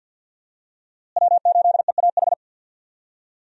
Morse code, heard as a keyed tone, about 46 wpm, 700 Hz. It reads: W8EAH
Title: USA Call Signs - 5 Characters - Repeated 2x - 50wpm